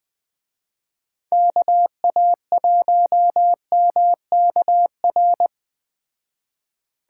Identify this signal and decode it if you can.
Morse code, heard as a keyed tone, about 20 wpm, 700 Hz. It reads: KA1MKR